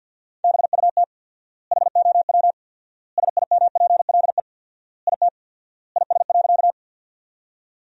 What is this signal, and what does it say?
Morse code, heard as a keyed tone, about 45 wpm, 700 Hz. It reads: BUT HOW SIMPLE IT IS.
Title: 4x Word-Spacing - Sentences from Top 400 Words - 50wpm